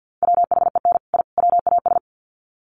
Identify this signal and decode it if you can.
Morse code, heard as a keyed tone, about 50 wpm, 700 Hz. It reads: WHEN I WAS
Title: Sets of 3 Words 50wpm